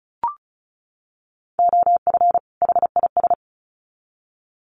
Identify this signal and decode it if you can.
Morse code, heard as a keyed tone, about 34 wpm, 700 Hz. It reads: OF HIS
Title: Sets of 2 Words 35wpm